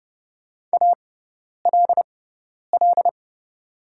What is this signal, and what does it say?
Morse code, heard as a keyed tone, about 30 wpm, spaced as for 10 wpm, 700 Hz. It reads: ALL
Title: 6x Character Spacing - Top 200 Words - 30wpm